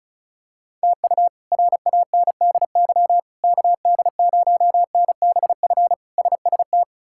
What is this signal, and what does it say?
Morse code, heard as a keyed tone, about 34 wpm, 700 Hz. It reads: TU RANDY KD0NBF SST
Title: K1USN SST QSO Practice - 35wpm